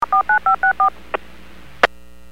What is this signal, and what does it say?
DTMF, heard as telephone touch tones, tones of about 110 ms. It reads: 49564
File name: WPS / 00953 08-20 WPS 451.25 WPS